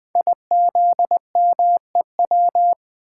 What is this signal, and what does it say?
Morse code, heard as a keyed tone, about 20 wpm, 700 Hz. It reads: IZMEW
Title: Code Groups - 5 Letters - 20wpm